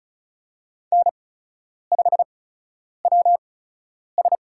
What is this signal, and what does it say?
Morse code, heard as a keyed tone, about 36 wpm, spaced as for 10 wpm, 700 Hz. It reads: N5WS
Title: USA Call Signs - 4 Characters - 8x Character Spacing - 35wpm